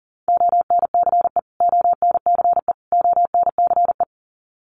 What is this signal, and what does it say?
Morse code, heard as a keyed tone, about 40 wpm, 700 Hz. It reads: ONCE ONCE ONCE
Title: Top 500 Words Repeated 40wpm